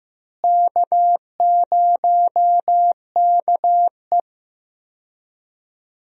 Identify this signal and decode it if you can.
Morse code, heard as a keyed tone, about 15 wpm, 700 Hz. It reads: K0KE
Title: USA Call Signs - 4 Characters - Repeated 2x - 15wpm